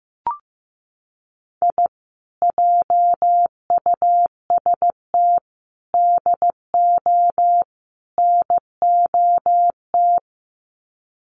Morse code, heard as a keyed tone, about 15 wpm, 700 Hz. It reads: I JUST DO NOT
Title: Sets of 4 Words 15wpm